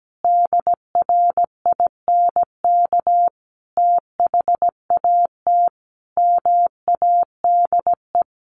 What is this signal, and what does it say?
Morse code, heard as a keyed tone, about 17 wpm, 700 Hz. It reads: DRINK THAT MADE
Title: Sentences from Top 1500 Words 17wpm